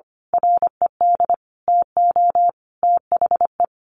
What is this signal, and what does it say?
Morse code, heard as a keyed tone, about 25 wpm, 700 Hz. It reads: RED TO THE
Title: A Sound Of Thunder 25wpm